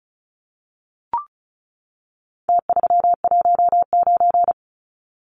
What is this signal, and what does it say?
Morse code, heard as a keyed tone, about 34 wpm, 700 Hz. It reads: T319
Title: Numbers - Contest Style - 4-Digit 35wpm